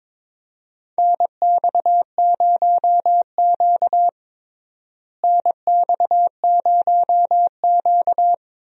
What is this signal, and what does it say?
Morse code, heard as a keyed tone, about 22 wpm, 700 Hz. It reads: NX0Q NX0Q
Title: USA Call Signs - 4 Characters - Repeated 3x - 22wpm